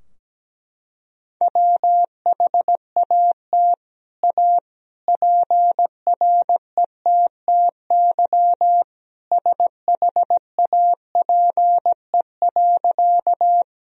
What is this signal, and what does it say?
Morse code, heard as a keyed tone, about 17 wpm, 700 Hz. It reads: WHAT A PRETTY SHAPE.
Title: Sentences from Top 1000 Words - Review Part 3 - 17wpm